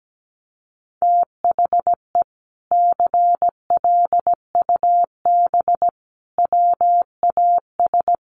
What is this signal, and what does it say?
Morse code, heard as a keyed tone, about 17 wpm, 700 Hz. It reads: THE CLUB WAS
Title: Sentences from Top 1200 Words 17wpm